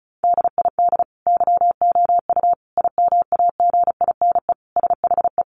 Morse code, heard as a keyed tone, about 34 wpm, 700 Hz. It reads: DID YOU IMAGINE SHE
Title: Sentences from Top 1400 Words 35wpm